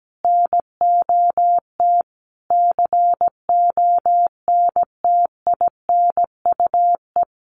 Morse code, heard as a keyed tone, about 17 wpm, 700 Hz. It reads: NOT CONTINUE
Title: Sentences from Top 1200 Words 17wpm